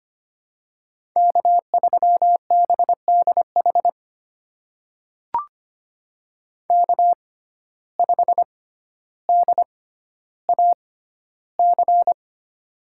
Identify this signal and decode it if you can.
Morse code, heard as a keyed tone, about 25 wpm, spaced as for 13 wpm, 700 Hz. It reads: K3BDH K5DAC
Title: USA Call Signs - 5 Characters - 6x Character Spacing - 25wpm